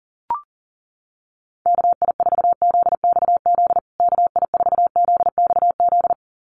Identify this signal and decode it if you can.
Morse code, heard as a keyed tone, about 37 wpm, 700 Hz. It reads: KI4ZXZ KI4ZXZ
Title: USA Call Signs Repeated 2x 40wpm